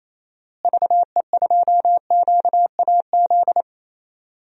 Morse code, heard as a keyed tone, about 28 wpm, 700 Hz. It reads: VE2QAZ
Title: DXCC-100 Call Signs - 6 Characters - Repeated 2x - 28wpm